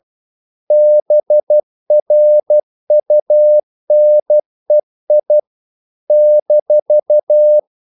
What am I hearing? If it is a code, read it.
Morse code, heard as a keyed tone, about 12 wpm, 600 Hz. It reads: BRUNEI -
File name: CW / Practice / NationalCapitals12